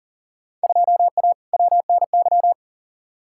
Morse code, heard as a keyed tone, about 40 wpm, 700 Hz. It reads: 2A WNY
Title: Field Day Exchanges - 40wpm